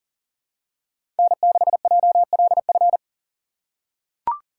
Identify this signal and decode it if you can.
Morse code, heard as a keyed tone, about 40 wpm, 700 Hz. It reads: N6JLF
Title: USA Call Signs - 3x Character Spacing - 40wpm